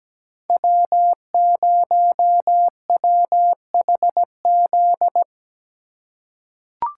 Morse code, heard as a keyed tone, about 17 wpm, 700 Hz. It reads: W0WHZ
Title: USA Call Signs - 5 Characters - 2x Character Spacing - 17wpm